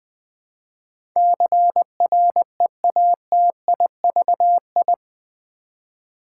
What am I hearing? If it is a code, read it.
Morse code, heard as a keyed tone, about 20 wpm, 700 Hz. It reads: CREATIVI
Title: Build A Word - 4 Letters At A Time - 10-Letter Words - 20wpm